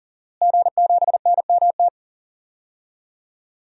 Morse code, heard as a keyed tone, about 40 wpm, 700 Hz. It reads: G7NMT
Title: DXCC-100 Call Signs - 5 Characters - Repeated 3x - 40wpm